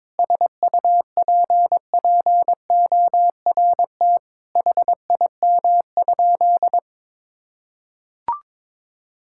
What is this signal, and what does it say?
Morse code, heard as a keyed tone, about 22 wpm, 700 Hz. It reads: SUPPORT HIM?
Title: Sentences from Top 1000 Words - Review Part 5 - 22wpm